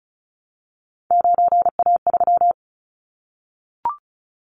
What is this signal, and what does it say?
Morse code, heard as a keyed tone, about 34 wpm, 700 Hz. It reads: 9A3
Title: Callsign Prefixes for Top 50 DXCC - 35wpm